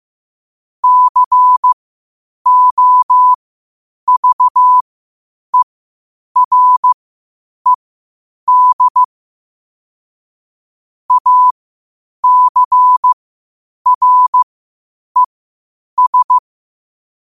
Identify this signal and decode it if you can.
Morse code, heard as a keyed tone, about 15 wpm, spaced as for 8 wpm, 1000 Hz. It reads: COVERED ACRES
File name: qotdfiles / 8wpm